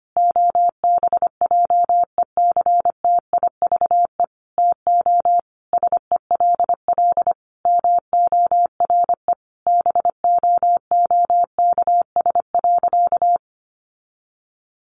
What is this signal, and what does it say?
Morse code, heard as a keyed tone, about 25 wpm, 700 Hz. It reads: OBJECTIVE TO SELL MORE BOOKS.
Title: Sentences from Top 1200 Words 25wpm